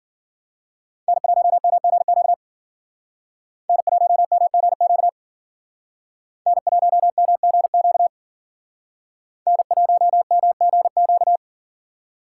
Morse code, heard as a keyed tone, about 52 wpm, 700 Hz. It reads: N1MGQ N1MGQ N1MGQ N1MGQ